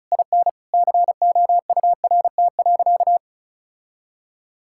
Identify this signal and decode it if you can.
Morse code, heard as a keyed tone, about 36 wpm, 700 Hz. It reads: IN COURT.
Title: Sentences from Top 1600 Words 35wpm